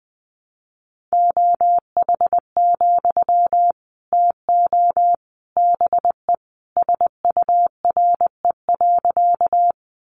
Morse code, heard as a keyed tone, about 20 wpm, 700 Hz. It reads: OH, TO BE SURE.
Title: Sentences from Top 500 Words 20wpm